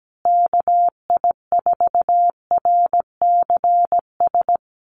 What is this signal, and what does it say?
Morse code, heard as a keyed tone, about 17 wpm, 700 Hz. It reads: KI4RCS
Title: USA Call Signs Repeated 2x 17wpm